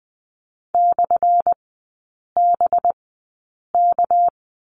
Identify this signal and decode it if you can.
Morse code, heard as a keyed tone, about 20 wpm, spaced as for 8 wpm, 700 Hz. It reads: /BK
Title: 218 Lesson 35e - Number 0 - Three Characters 20wpm